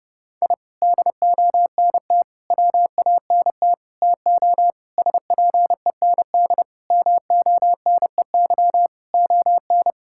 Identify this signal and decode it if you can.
Morse code, heard as a keyed tone, about 29 wpm, 700 Hz. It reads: I DONT WANT TO SPEND MONEY ON
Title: Sentences from Top 1000 Words - Review Part 5 - 30wpm